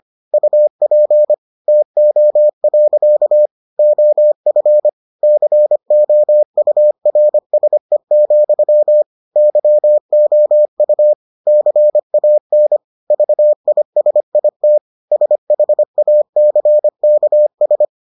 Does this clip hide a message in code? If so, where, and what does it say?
Yes: Morse code from 0.3 to 17.9 s, heard as a keyed tone, about 25 wpm, 600 Hz. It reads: UP TO. OF COURSE, YOU CAN VISIT SHACKS